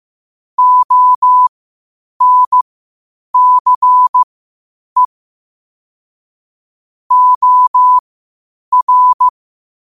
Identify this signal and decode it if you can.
Morse code, heard as a keyed tone, about 15 wpm, spaced as for 8 wpm, 1000 Hz. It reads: ONCE OR
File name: 8wpm